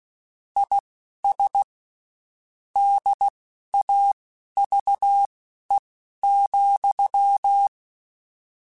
Morse code, heard as a keyed tone, about 16 wpm, spaced as for 11 wpm, 800 Hz. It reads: IS DAVE,